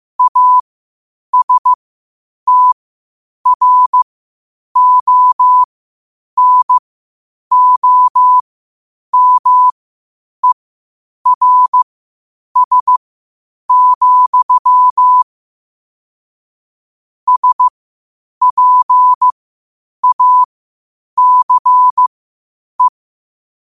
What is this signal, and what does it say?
Morse code, heard as a keyed tone, about 15 wpm, spaced as for 8 wpm, 1000 Hz. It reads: ASTRONOMERS, SPACE
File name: qotdfiles / 8wpm